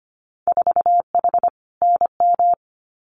Morse code, heard as a keyed tone, about 25 wpm, 700 Hz. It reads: 4H NM
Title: Winter Field Day Exchanges Encore 25wpm